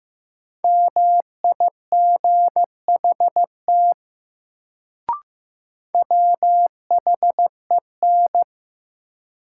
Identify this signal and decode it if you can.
Morse code, heard as a keyed tone, about 15 wpm, 700 Hz. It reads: MIGHT WHEN
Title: Sets of 2 Words - New Word Drill - 15wpm